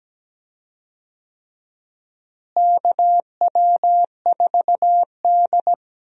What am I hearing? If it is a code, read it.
Morse code, heard as a keyed tone, about 17 wpm, 700 Hz. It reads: KW4D
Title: USA Call Signs - 4 Characters - Repeated 3x - 17wpm